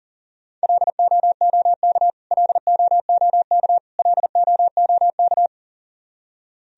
Morse code, heard as a keyed tone, about 40 wpm, 700 Hz. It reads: LOOK LOOK LOOK
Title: Top 100 Words Repeated 40wpm